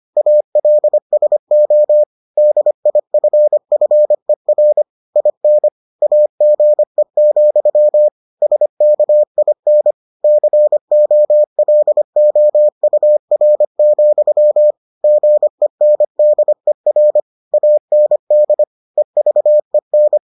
Morse code, heard as a keyed tone, about 25 wpm, 600 Hz. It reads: ALSO DIFFER IN AGE, SKIN COLOUR, GENDER AND EVEN